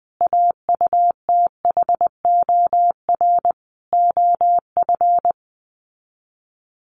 Morse code, heard as a keyed tone, about 20 wpm, 700 Hz. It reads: AUTHOR OF
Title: Sets of 5 Words Encore2 20wpm